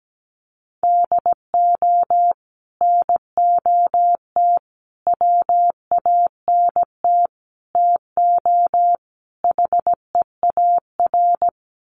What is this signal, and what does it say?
Morse code, heard as a keyed tone, about 17 wpm, 700 Hz. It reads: DO NOT WANT TO HEAR
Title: Sets of 5 Words 17wpm